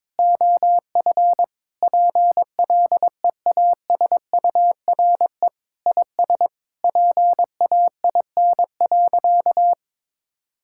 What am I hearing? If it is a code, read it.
Morse code, heard as a keyed tone, about 22 wpm, 700 Hz. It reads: OF PLEASURE IS PAIN.
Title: Sentences from Top 1600 Words 22wpm